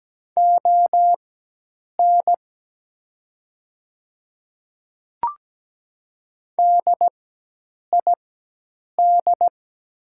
Morse code, heard as a keyed tone, about 17 wpm, spaced as for 8 wpm, 700 Hz. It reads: ON DID
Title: 4x Character Spacing - Top 100 Words - 17wpm